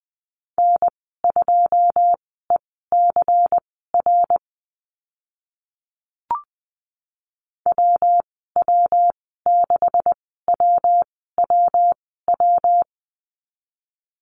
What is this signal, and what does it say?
Morse code, heard as a keyed tone, about 20 wpm, spaced as for 14 wpm, 700 Hz. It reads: N2ECR WW6WWW
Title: 140 Lesson 23g - Letter f - Call Signs 20wpm